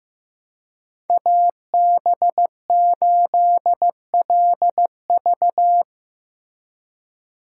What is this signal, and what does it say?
Morse code, heard as a keyed tone, about 18 wpm, 700 Hz. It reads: AB8LV
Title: Speed Racing - USA Call Signs 15wpm